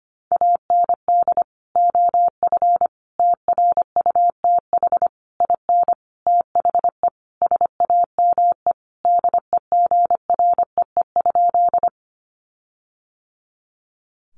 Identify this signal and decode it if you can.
Morse code, heard as a keyed tone, about 25 wpm, 700 Hz. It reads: AND OF TRUTH IN THE SAME DEGREE?
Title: Sentences from Top 1500 Words 25wpm